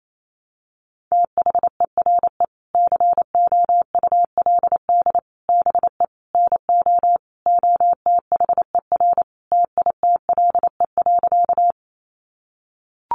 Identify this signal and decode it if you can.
Morse code, heard as a keyed tone, about 28 wpm, 700 Hz. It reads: THERE COULD BE NO OTHER TITLE.
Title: Sentences from Top 1200 Words 28wpm